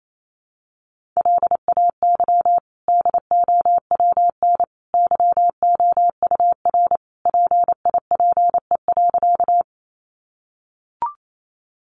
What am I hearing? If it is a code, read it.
Morse code, heard as a keyed tone, about 28 wpm, 700 Hz. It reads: LAY DOWN YOUR PIPE.
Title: Sentences from Top 1500 Words 28wpm